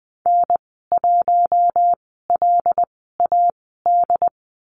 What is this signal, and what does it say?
Morse code, heard as a keyed tone, about 20 wpm, spaced as for 15 wpm, 700 Hz. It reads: N1LAD
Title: 082 Lesson 14g - Number 2 - Call Signs 20wpm